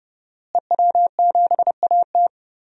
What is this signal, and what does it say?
Morse code, heard as a keyed tone, about 30 wpm, 700 Hz. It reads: EW7AT
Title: DXCC-100 Call Signs - 5 Characters - Repeated 3x - 30wpm